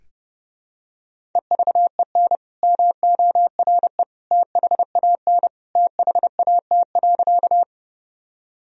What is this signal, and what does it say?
Morse code, heard as a keyed tone, about 30 wpm, 700 Hz. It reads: EVEN MORE THAN THAT.
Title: Sentences from Top 1000 Words - Review Part 1 - 30wpm